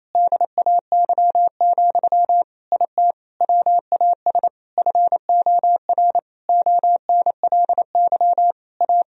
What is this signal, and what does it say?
Morse code, heard as a keyed tone, about 28 wpm, 700 Hz. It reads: DAY, IT WAS FOR ONLY A
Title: Sentences from Top 1000 Words - Review Part 3 - 28wpm